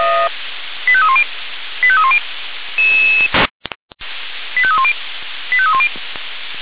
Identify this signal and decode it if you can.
Five-tone selective call (ZVEI), heard as quick runs of five tones, tones of about 70 ms. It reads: 85310 85310 85310 85310
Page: /496/SDR/ZVEI/85310/2023.06.12/